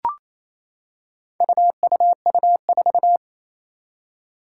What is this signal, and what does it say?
Morse code, heard as a keyed tone, about 27 wpm, 700 Hz. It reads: UUU4
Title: Four Challenging Letter-Number Combinations 28wpm